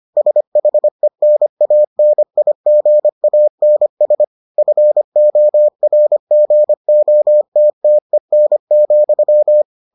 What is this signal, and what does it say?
Morse code, heard as a keyed tone, about 25 wpm, 600 Hz. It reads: SHENANIGANS FORGOTTEN,